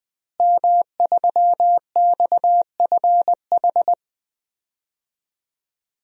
Morse code, heard as a keyed tone, about 20 wpm, 700 Hz. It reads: M3XFH
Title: DXCC-100 Call Signs - 5 Characters - Repeated 2x - 20wpm